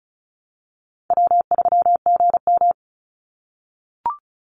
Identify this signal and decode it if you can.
Morse code, heard as a keyed tone, about 36 wpm, 700 Hz. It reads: W3GM
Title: USA Call Signs 35wpm